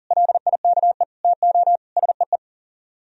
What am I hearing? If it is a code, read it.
Morse code, heard as a keyed tone, about 51 wpm, 700 Hz. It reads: LIKE TO SEE